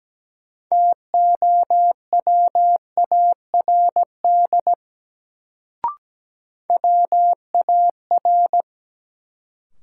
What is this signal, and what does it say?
Morse code, heard as a keyed tone, about 17 wpm, 700 Hz. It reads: TOWARD WAR
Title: Sets of 5 Words - New Word Drill - 17wpm